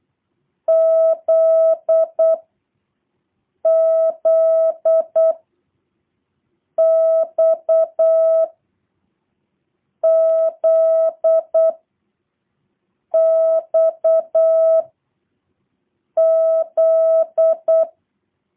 Morse code, heard as a keyed tone, about 8 wpm, 650 Hz. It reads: Z Z X Z X Z